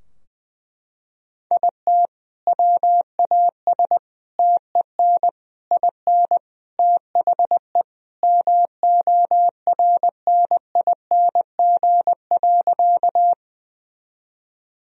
Morse code, heard as a keyed tone, about 20 wpm, 700 Hz. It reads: IT WAS TEN IN THE MORNING.
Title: Sentences from Top 1000 Words - Review Part 2 - 20wpm